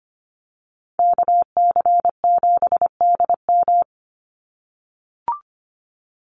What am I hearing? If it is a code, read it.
Morse code, heard as a keyed tone, about 25 wpm, 700 Hz. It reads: KC7DM
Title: NA SOTA Activator Callsigns 25wpm